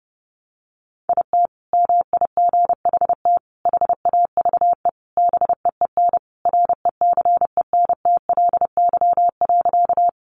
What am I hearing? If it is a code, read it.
Morse code, heard as a keyed tone, about 30 wpm, 700 Hz. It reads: IT MIGHT HAVE BEEN RECENTLY.
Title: Sentences from Top 1200 Words 30wpm